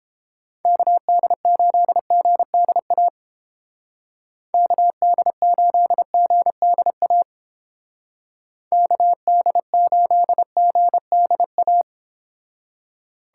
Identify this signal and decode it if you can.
Morse code, heard as a keyed tone, about 28 wpm, 700 Hz. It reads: KD8GDA KD8GDA KD8GDA